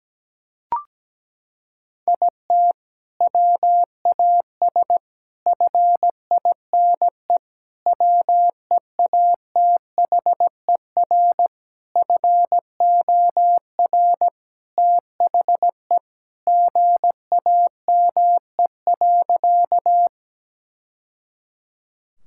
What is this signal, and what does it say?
Morse code, heard as a keyed tone, about 17 wpm, 700 Hz. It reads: IT WAS FINE WEATHER FOR THE GAME.
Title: Sentences from Top 1000 Words - Review Part 5 - 17wpm